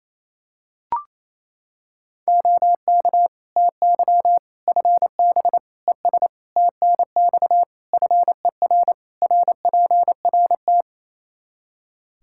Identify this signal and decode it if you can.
Morse code, heard as a keyed tone, about 28 wpm, 700 Hz. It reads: OK TY FB ES TNX FER RPRT
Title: K1USN SST QSO Practice 28wpm